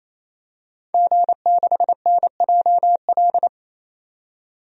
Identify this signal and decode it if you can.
Morse code, heard as a keyed tone, about 28 wpm, 700 Hz. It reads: G6NJL
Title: DXCC-100 Call Signs - 5 Characters - 28wpm